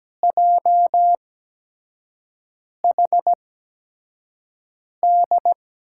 Morse code, heard as a keyed tone, about 17 wpm, spaced as for 5 wpm, 700 Hz. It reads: JHD
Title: USA Call Signs - 6 Characters - 8x Character Spacing - 17wpm